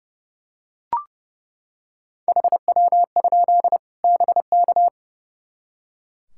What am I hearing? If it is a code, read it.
Morse code, heard as a keyed tone, about 30 wpm, 700 Hz. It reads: HW? BK
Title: Practice QSO - Exchange 2 - Info - 30wpm